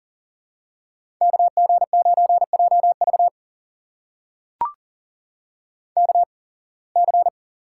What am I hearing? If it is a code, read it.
Morse code, heard as a keyed tone, about 38 wpm, spaced as for 25 wpm, 700 Hz. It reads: KG9JV KC
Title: USA Call Signs - 8x Character Spacing - 40wpm